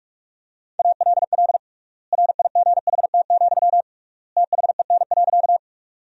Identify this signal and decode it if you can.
Morse code, heard as a keyed tone, about 46 wpm, 700 Hz. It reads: ALL RIGHT, THEN.
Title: 3x Word-Spacing - Sentences from Top 200 Words - 45wpm